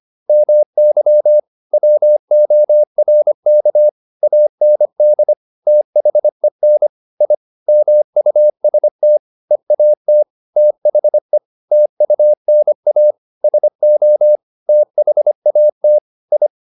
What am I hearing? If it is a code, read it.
Morse code, heard as a keyed tone, about 25 wpm, 600 Hz. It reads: MY WORK AND THEN I MUST EAT THE TUNA SO THAT I
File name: the old man and the sea part 1 25wpm